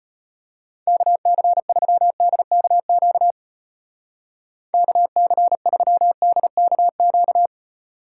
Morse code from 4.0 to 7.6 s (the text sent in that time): KC3DKQ